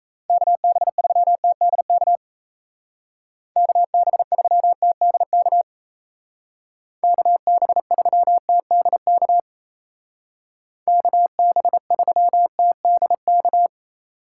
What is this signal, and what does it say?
Morse code, heard as a keyed tone, about 34 wpm, 700 Hz. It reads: KB3TDK KB3TDK KB3TDK KB3TDK
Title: Speed Racing - USA 2x3 Call Signs 28wpm